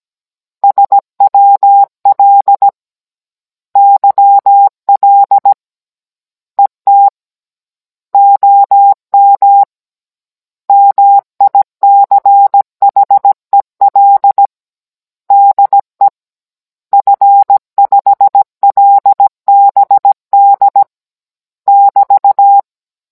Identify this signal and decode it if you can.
Morse code, heard as a keyed tone, about 17 wpm, 800 Hz. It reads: SWL YL ET OM MICHEL DE F5LBD <BT>